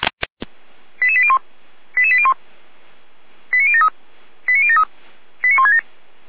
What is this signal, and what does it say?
Five-tone selective call (ZVEI), heard as quick runs of five tones, tones of about 70 ms. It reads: 88801 88801 89073 89073 89167